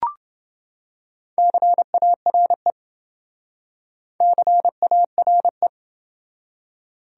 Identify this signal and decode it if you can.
Morse code, heard as a keyed tone, about 26 wpm, 700 Hz. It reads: CARE CARE